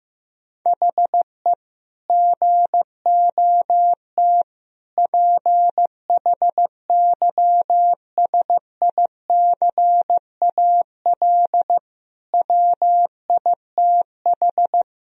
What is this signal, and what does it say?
Morse code, heard as a keyed tone, about 15 wpm, 700 Hz. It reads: HE GOT PHYSICAL WITH
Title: Sentences from Top 1000 Words - Review Part 4 - 15wpm